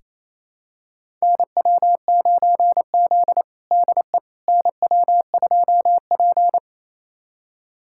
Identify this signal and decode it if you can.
Morse code, heard as a keyed tone, about 28 wpm, 700 Hz. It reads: NW9Z DE NW2P
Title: USA Callsign DE Callsign - 4 Characters - 28wpm